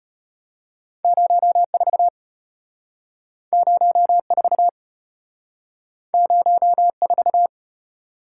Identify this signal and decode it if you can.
Morse code, heard as a keyed tone, about 32 wpm, 700 Hz. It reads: 04 04 04